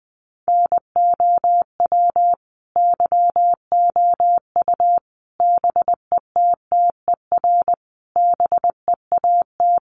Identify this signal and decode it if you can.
Morse code, heard as a keyed tone, about 20 wpm, 700 Hz. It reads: NOW YOU BETTER BEAT
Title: Sentences from Top 1300 Words 20wpm